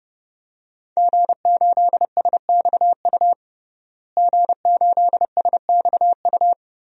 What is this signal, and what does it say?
Morse code, heard as a keyed tone, about 30 wpm, 700 Hz. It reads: G8SXU G8SXU
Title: DXCC-100 Call Signs - 5 Characters - Repeated 2x - 30wpm